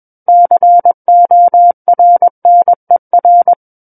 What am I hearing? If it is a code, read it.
Morse code, heard as a keyed tone, about 21 wpm, 700 Hz. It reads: CORNER
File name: Book One / 12-Chapter 12 - 21 WPM